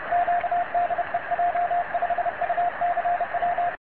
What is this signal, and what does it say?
Morse code, heard as a keyed tone, about 30 wpm, 700 Hz, with radio noise. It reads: Q DE J5UCW
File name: j5ucw-cq-cw